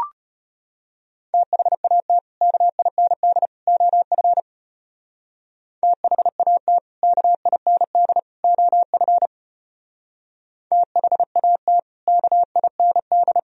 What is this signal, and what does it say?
Morse code, heard as a keyed tone, about 32 wpm, 700 Hz. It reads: THAT KIND OF THAT KIND OF THAT KIND